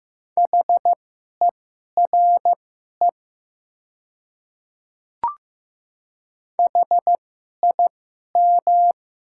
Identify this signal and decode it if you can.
Morse code, heard as a keyed tone, about 15 wpm, spaced as for 11 wpm, 700 Hz. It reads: HERE HIM
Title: 2x Character Spacing - Top 200 Words - 15wpm